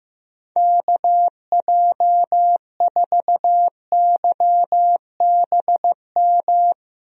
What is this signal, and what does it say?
Morse code, heard as a keyed tone, about 15 wpm, 700 Hz. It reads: KJ4YBM
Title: USA Call Signs - 6 Characters - Repeated 2x - 15wpm